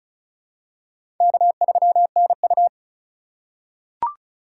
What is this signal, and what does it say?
Morse code, heard as a keyed tone, about 36 wpm, 690 Hz. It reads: K3NU
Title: USA Call Signs - 4 Characters - 3x Character Spacing - 35wpm